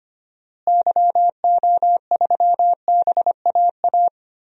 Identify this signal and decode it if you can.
Morse code, heard as a keyed tone, about 25 wpm, 700 Hz. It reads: YO3BAA
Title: DXCC-100 Call Signs - 6 Characters - Repeated 3x - 25wpm